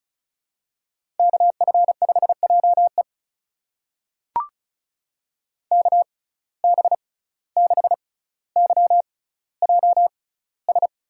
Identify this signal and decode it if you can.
Morse code, heard as a keyed tone, about 36 wpm, spaced as for 17 wpm, 700 Hz. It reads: KF5JE KB6YJS
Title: USA Call Signs - 6x Character Spacing - 35wpm